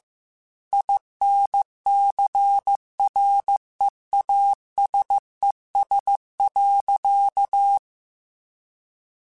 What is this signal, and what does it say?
Morse code, heard as a keyed tone, about 15 wpm, 800 Hz. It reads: INCREASES.
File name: Technical Subjects / Technical Subjects-mp3s / XL @13